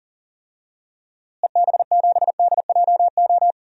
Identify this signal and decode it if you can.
Morse code, heard as a keyed tone, about 40 wpm, 700 Hz. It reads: EB7DJO
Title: DXCC-100 Call Signs - 6 Characters - Repeated 2x - 40wpm